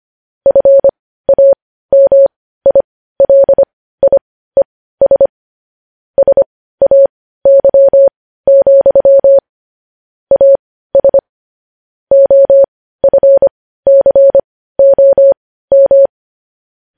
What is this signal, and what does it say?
Morse code, heard as a keyed tone, about 25 wpm, spaced as for 15 wpm, 550 Hz. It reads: FAMILIES SAY, AS OFCOM
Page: news-headlines-in-morse-code-at-15-wpm